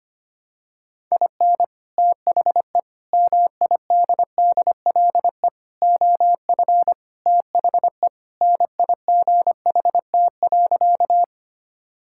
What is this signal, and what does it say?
Morse code, heard as a keyed tone, about 25 wpm, 700 Hz. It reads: IN THE MIDDLE OF THE NIGHT.
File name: practice / 25wpm / Sentences with 3-Word Phrases - 25wpm